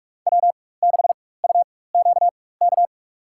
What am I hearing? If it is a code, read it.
Morse code, heard as a keyed tone, about 45 wpm, spaced as for 21 wpm, 700 Hz. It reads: W6UQK
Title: USA Call Signs - 5 Characters - 4x Character Spacing - 45wpm